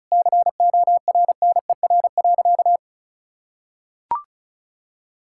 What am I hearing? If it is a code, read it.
Morse code, heard as a keyed tone, about 34 wpm, 700 Hz. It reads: CORNER.
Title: Sentences from Top 1600 Words 35wpm